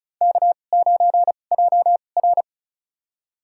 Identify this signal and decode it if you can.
Morse code, heard as a keyed tone, about 36 wpm, spaced as for 25 wpm, 700 Hz. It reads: K9JR
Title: USA Call Signs - 4 Characters - 2x Character Spacing - 35wpm